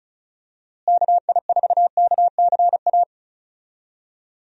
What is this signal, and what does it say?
Morse code, heard as a keyed tone, about 34 wpm, 700 Hz. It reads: KI4KCA
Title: USA Call Signs - 6 Characters - Repeated 3x - 35wpm